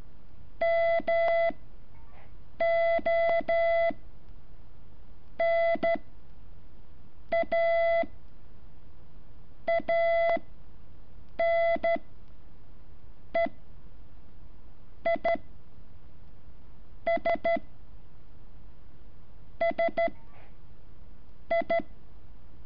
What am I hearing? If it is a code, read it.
Morse code, heard as a keyed tone, about 13 wpm, 670 Hz. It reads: M O N A A N E I S S I
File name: curso / ejercicio 11